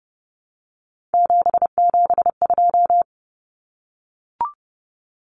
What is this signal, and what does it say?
Morse code, heard as a keyed tone, about 30 wpm, 700 Hz. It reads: 772
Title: Numbers - 3-Digit 30wpm